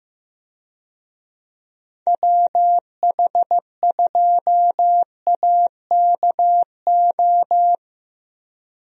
Morse code, heard as a keyed tone, about 15 wpm, 700 Hz. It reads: WH2AKO
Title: USA Call Signs - 6 Characters - Repeated 2x - 15wpm